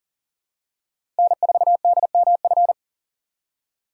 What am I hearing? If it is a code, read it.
Morse code, heard as a keyed tone, about 40 wpm, 700 Hz. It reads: N4DMF
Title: USA Call Signs - 5 Characters - 6x Character Spacing - 40wpm